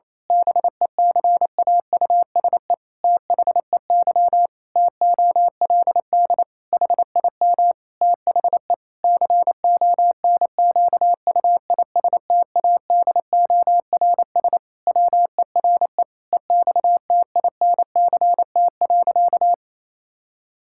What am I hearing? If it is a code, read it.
Morse code, heard as a keyed tone, about 28 wpm, 700 Hz. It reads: BECAUSE THEY TOLD HIM THE CONQUISTADORS WERE EXTINCT.
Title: Pangram Challenge v2 - 28wpm